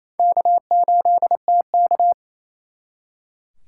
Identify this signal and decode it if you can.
Morse code, heard as a keyed tone, about 36 wpm, 700 Hz. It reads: K8TK